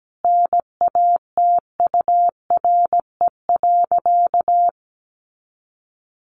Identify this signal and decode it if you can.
Morse code, heard as a keyed tone, about 17 wpm, 700 Hz. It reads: NATURE.
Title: Sentences from Top 1300 Words 17wpm